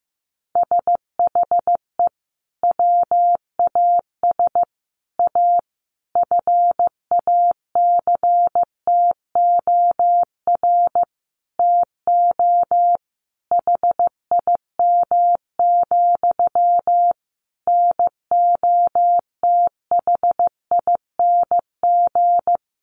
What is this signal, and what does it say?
Morse code, heard as a keyed tone, about 15 wpm, 700 Hz. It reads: SHE WAS A FACTOR TO HIM, NOTHING
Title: Sentences from Top 1300 Words 15wpm